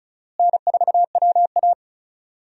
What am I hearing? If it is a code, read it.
Morse code, heard as a keyed tone, about 36 wpm, 700 Hz. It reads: N4WA
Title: USA Call Signs - 4 Characters - Repeated 3x - 35wpm